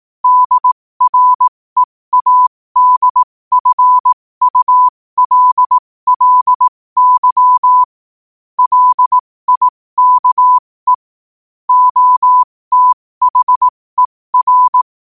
Morse code, heard as a keyed tone, about 18 wpm, spaced as for 15 wpm, 1000 Hz. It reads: DREADFULLY LIKE OTHER
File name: qotdfiles / 15wpm